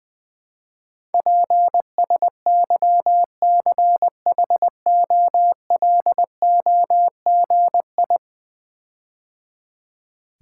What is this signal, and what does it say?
Morse code, heard as a keyed tone, about 21 wpm, 700 Hz. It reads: PSYCHOLOGI
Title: Build A Word - 5 Letters At A Time - 12-Letter Words - 20wpm